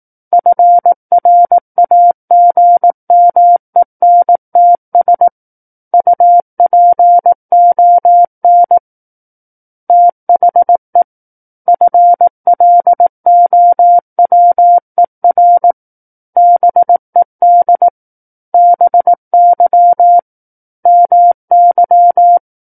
Morse code, heard as a keyed tone, about 18 wpm, 700 Hz. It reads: FRAGMENTS UPON THE FLOWER BED BY MY
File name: Book One / 09-Chapter 9 - 18 WPM